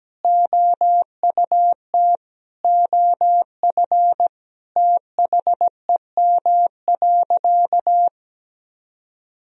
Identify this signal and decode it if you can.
Morse code, heard as a keyed tone, about 17 wpm, 700 Hz. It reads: OUT OF THEM.
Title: Sentences from Top 1000 Words - Review Part 3 - 17wpm